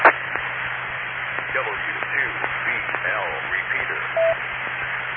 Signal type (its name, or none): Emergency Alert System (SAME) tones